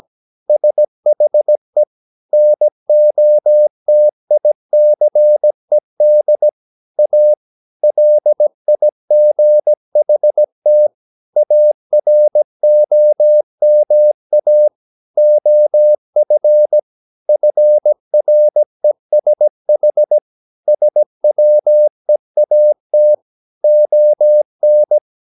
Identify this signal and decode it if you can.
Morse code, heard as a keyed tone, about 17 wpm, 600 Hz. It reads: SHE NOTICED A LIGHT AROMA OF FRESH SWEAT ON
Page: Romanen Never af Ken Follet på engelsk.